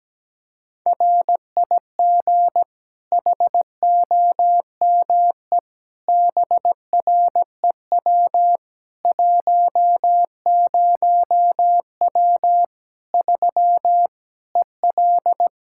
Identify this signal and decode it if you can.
Morse code, heard as a keyed tone, about 17 wpm, 700 Hz. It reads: RIG HOME BREW 10W 3 EL
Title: K1USN SST QSO Practice 17wpm